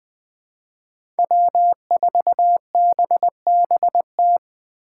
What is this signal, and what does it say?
Morse code, heard as a keyed tone, about 21 wpm, 700 Hz. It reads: W4BBT
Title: K1USN SST QSO Practice - 20wpm